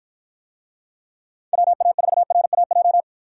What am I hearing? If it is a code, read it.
Morse code, heard as a keyed tone, about 49 wpm, 700 Hz. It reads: WA4RAJ